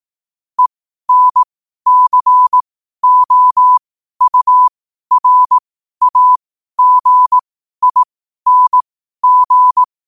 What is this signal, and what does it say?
Morse code, heard as a keyed tone, about 18 wpm, spaced as for 12 wpm, 1000 Hz. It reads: ENCOURAGING
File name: qotdfiles / 12wpm